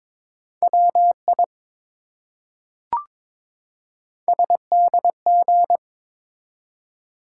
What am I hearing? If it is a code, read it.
Morse code, heard as a keyed tone, about 22 wpm, 700 Hz. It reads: WI SDG
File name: ARRL Sections v2 - 22wpm